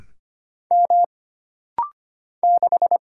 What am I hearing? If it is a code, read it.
Morse code, heard as a keyed tone, about 25 wpm, 700 Hz. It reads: M 6
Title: Single Letter-Number - Rapid-Fire 25wpm